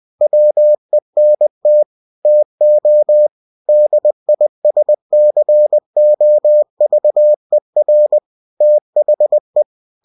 Morse code, heard as a keyed tone, about 20 wpm, 600 Hz. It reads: WENT TO DISCOVER THE
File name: winnie-20wpm